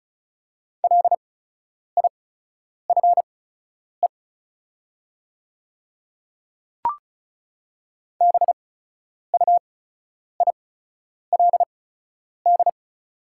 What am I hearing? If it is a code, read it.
Morse code, heard as a keyed tone, about 34 wpm, spaced as for 10 wpm, 700 Hz. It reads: LIFE BUILD
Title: 8x Character Spacing - Top 500 Words - 35wpm